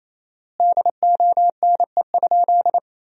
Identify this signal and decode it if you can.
Morse code, heard as a keyed tone, about 28 wpm, 700 Hz. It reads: DONE?
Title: 4x Word-Spacing - Sentences from Top 500 Words - 28wpm